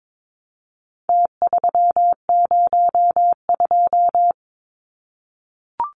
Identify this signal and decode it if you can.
Morse code, heard as a keyed tone, about 22 wpm, 700 Hz. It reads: T302
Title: Numbers - Contest Style - 4-Digit 22wpm